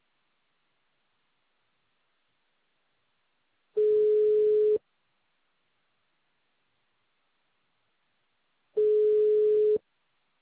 Ringback tone (Europe, 425 Hz).